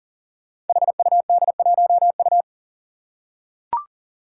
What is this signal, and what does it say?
Morse code, heard as a keyed tone, about 40 wpm, 700 Hz. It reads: HUD1U
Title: Five Challenging Letter-Number Combinations 40wpm